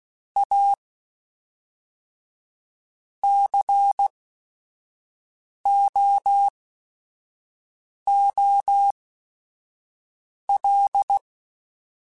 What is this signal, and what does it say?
Morse code, heard as a keyed tone, about 16 wpm, spaced as for 5 wpm, 800 Hz. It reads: A COOL
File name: Weather Sets / Weather Sets-mp3s / Weather 5 1 @5